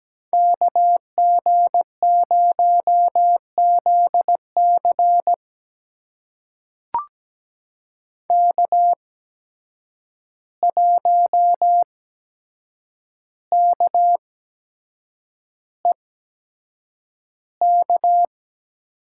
Morse code, heard as a keyed tone, about 17 wpm, spaced as for 7 wpm, 700 Hz. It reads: KG0ZC K1KEK
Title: USA Call Signs - 5 Characters - 8x Character Spacing - 17wpm